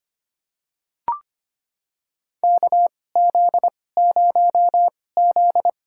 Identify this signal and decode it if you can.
Morse code, heard as a keyed tone, about 25 wpm, spaced as for 18 wpm, 700 Hz. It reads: KZ0Z
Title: USA Call Signs - 4 Characters - 2x Character Spacing - 25wpm